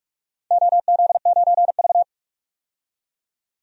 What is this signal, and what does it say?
Morse code, heard as a keyed tone, about 44 wpm, 700 Hz. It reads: OZ9V
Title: DXCC-100 Call Signs - 4 Characters - 45wpm